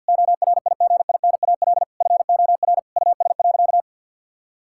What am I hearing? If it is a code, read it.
Morse code, heard as a keyed tone, about 50 wpm, 700 Hz. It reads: ORIGINAL FOR US.
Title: Sentences from Top 1000 Words - Review Part 6 - 50wpm